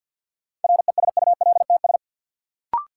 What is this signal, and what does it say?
Morse code, heard as a keyed tone, about 53 wpm, 700 Hz. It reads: RESULTS
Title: Sets of 4 Words - New Word Drill - 50wpm